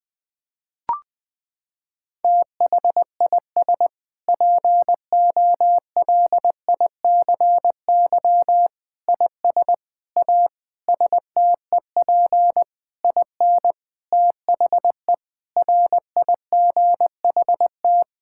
Morse code, heard as a keyed tone, about 20 wpm, 700 Hz. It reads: THIS POLICY IS A STEP IN THE RIGHT
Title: Sentences with 6-Word Phrases - 20wpm